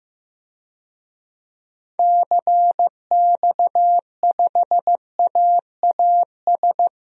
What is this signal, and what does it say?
Morse code, heard as a keyed tone, about 15 wpm, 700 Hz. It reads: CX5AAS
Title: DXCC-100 Call Signs - 6 Characters - Repeated 3x - 15wpm